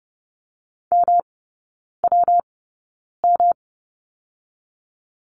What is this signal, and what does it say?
Morse code, heard as a keyed tone, about 30 wpm, spaced as for 9 wpm, 700 Hz. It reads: MWM
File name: Lesson 19e - Letter w - Three Characters 30wpm